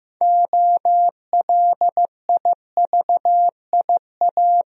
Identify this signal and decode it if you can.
Morse code, heard as a keyed tone, about 15 wpm, 700 Hz. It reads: OLIVIA
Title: USA Common Names - 15wpm